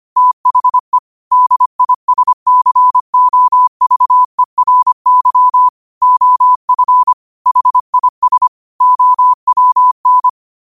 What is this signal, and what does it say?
Morse code, heard as a keyed tone, about 25 wpm, 1000 Hz. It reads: THE DISCOVERY OF HIS OWN